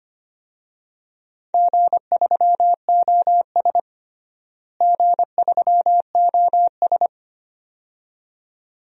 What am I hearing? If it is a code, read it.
Morse code, heard as a keyed tone, about 25 wpm, 700 Hz. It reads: G3OS G3OS
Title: DXCC-100 Call Signs - 4 Characters - Repeated 2x - 25wpm